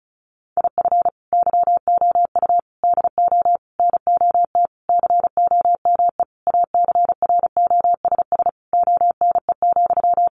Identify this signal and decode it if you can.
Morse code, heard as a keyed tone, about 34 wpm, 700 Hz. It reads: IF YOU DO NOT COME ACROSS ONE,
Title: Sentences from Top 800 Words 35wpm